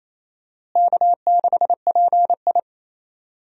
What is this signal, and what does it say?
Morse code, heard as a keyed tone, about 28 wpm, 700 Hz. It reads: K6PI
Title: USA Call Signs - 4 Characters - Repeated 2x - 28wpm